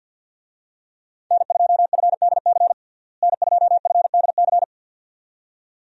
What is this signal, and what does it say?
Morse code, heard as a keyed tone, about 50 wpm, 700 Hz. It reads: N2FDC N2FDC
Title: USA Call Signs - 5 Characters - Repeated 2x - 50wpm